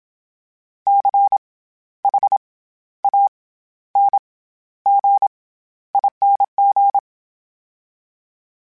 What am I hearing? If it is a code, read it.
Morse code, heard as a keyed tone, about 27 wpm, 800 Hz. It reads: C H A N G ING
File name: Special INGs-mp3s / Special INGs-mp3s / SlowToFaster ing @25